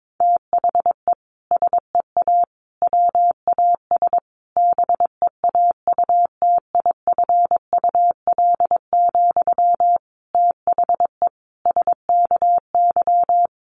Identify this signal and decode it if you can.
Morse code, heard as a keyed tone, about 22 wpm, 700 Hz. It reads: THE SEA WAS BEAUTIFUL, THE SKY
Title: Sentences from Top 1200 Words 22wpm